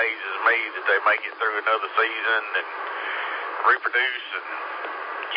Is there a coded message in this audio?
Yes: Emergency Alert System (SAME) tones from 0.5 to 4.2 s, heard as harsh buzzing bursts, with speech.